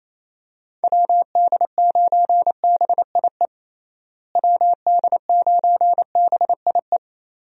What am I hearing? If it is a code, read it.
Morse code, heard as a keyed tone, about 27 wpm, 700 Hz. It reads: WD9BIE WD9BIE
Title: USA Call Signs - 6 Characters - Repeated 3x - 28wpm